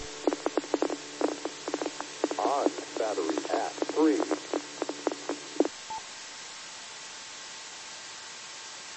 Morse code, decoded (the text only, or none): none